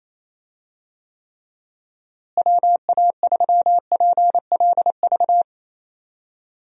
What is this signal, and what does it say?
Morse code, heard as a keyed tone, about 28 wpm, 700 Hz. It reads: WA3PLV
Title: USA Call Signs - 6 Characters - Repeated 2x - 28wpm